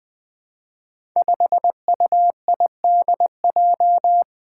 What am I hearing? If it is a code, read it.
Morse code, heard as a keyed tone, about 20 wpm, 700 Hz. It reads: 5UIDJ
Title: Five Challenging Letter-Number Combinations 20wpm